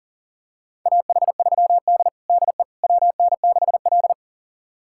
Morse code, heard as a keyed tone, about 40 wpm, 700 Hz. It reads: AH3D DE WN6L
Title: USA Callsign DE Callsign - 4 Characters - 40wpm